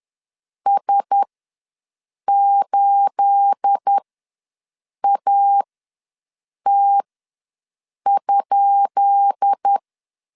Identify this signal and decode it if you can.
Morse code, heard as a keyed tone, about 10 wpm, spaced as for 6 wpm, 800 Hz. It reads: S8AT?
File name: LEZIONI / 10 SIGNS / 10B INT ws 10wpm 4134